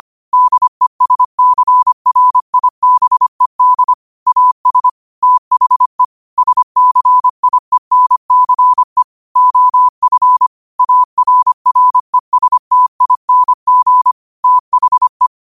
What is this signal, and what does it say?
Morse code, heard as a keyed tone, about 25 wpm, 1000 Hz. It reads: DESCRIBED AS THE SCIENCE OF ARRESTING THE
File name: qotdfiles / 25wpm